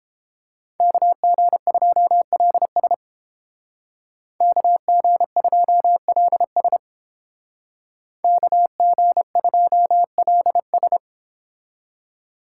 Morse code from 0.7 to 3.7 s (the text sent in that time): KG2LS